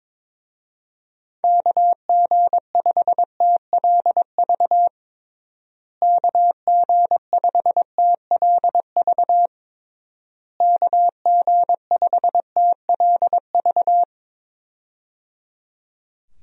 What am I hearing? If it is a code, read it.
Morse code, heard as a keyed tone, about 22 wpm, 700 Hz. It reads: KG5TLV KG5TLV KG5TLV
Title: USA Call Signs - 6 Characters - Repeated 3x - 22wpm